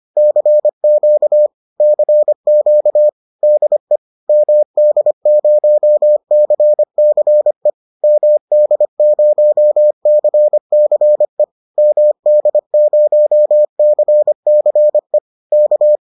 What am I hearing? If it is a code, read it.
Morse code, heard as a keyed tone, about 25 wpm, 600 Hz. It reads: CQ CQ DE MD0CCE MD0CCE MD0CCE K
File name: 25wpm-6